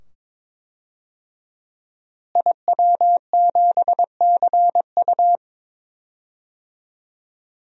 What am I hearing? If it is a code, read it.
Morse code, heard as a keyed tone, about 21 wpm, 700 Hz. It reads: IW7CU
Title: DXCC-100 Call Signs - 5 Characters - Repeated 2x - 22wpm